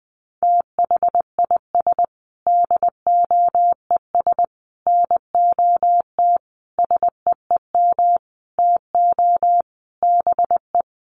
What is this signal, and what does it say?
Morse code, heard as a keyed tone, about 20 wpm, 700 Hz. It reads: THIS DOES NOT SEEM TO BE
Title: Sentences from Top 1400 Words 20wpm